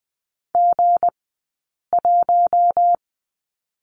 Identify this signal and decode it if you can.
Morse code, heard as a keyed tone, about 20 wpm, spaced as for 8 wpm, 700 Hz. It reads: G1
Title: 171 Lesson 28d - Number 9 - Two Characters 20wpm